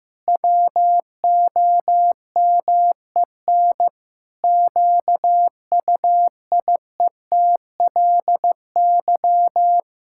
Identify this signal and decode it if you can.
Morse code, heard as a keyed tone, about 16 wpm, 700 Hz. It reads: WOMEN QUIETLY
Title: Pangram Challenge v2 - 15wpm